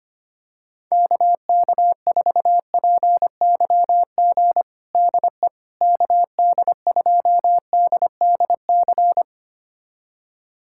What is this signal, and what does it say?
Morse code, heard as a keyed tone, about 25 wpm, 700 Hz. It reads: KK4PYG DE KD2DDC
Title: USA Callsign DE Callsign 25wpm